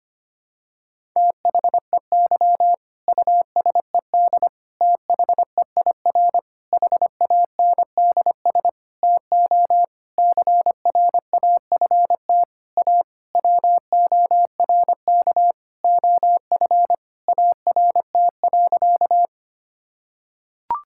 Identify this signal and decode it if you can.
Morse code, heard as a keyed tone, about 25 wpm, 700 Hz. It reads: THEY USED THEIR HANDS TO CRAFT A WORK OF ART.
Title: Sentences from Top 1000 Words - Review Part 6 - 25wpm